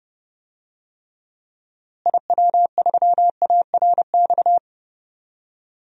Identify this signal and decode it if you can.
Morse code, heard as a keyed tone, about 30 wpm, 700 Hz. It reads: IW3ARX
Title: DXCC-100 Call Signs - 6 Characters - Repeated 3x - 30wpm